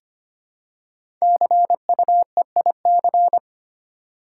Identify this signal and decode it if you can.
Morse code, heard as a keyed tone, about 25 wpm, 700 Hz. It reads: CUEIC
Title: Code Groups - 5 Letter:Numbers - 25wpm